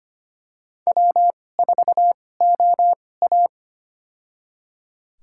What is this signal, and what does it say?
Morse code, heard as a keyed tone, about 25 wpm, spaced as for 18 wpm, 700 Hz. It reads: W4OA
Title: USA Call Signs - 2x Character Spacing - 25wpm